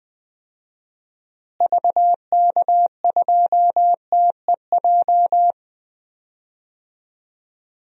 Morse code, heard as a keyed tone, about 20 wpm, 700 Hz. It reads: VK2TEJ
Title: DXCC-100 Call Signs - 6 Characters - Repeated 2x - 20wpm